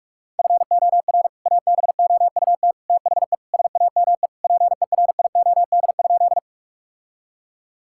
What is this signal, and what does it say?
Morse code, heard as a keyed tone, about 46 wpm, 700 Hz. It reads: FOR ABOUT THE SAME PERIOD?
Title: Sentences from Top 1000 Words - Review Part 5 - 45wpm